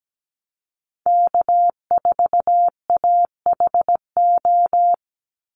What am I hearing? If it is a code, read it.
Morse code, heard as a keyed tone, about 17 wpm, 700 Hz. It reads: K4AHO
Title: Call Signs Repeated 2x - 17wpm